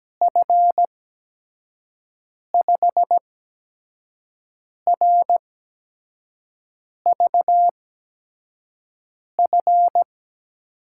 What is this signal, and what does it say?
Morse code, heard as a keyed tone, about 17 wpm, spaced as for 5 wpm, 700 Hz. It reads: F5RVF
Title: USA Call Signs - 6 Characters - 8x Character Spacing - 17wpm